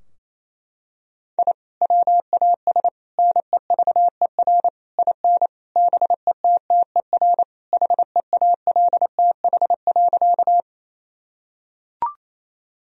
Morse code, heard as a keyed tone, about 28 wpm, 700 Hz. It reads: I WAS NEVER IN BETTER HEALTH.
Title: Sentences from Top 1000 Words - Review Part 3 - 28wpm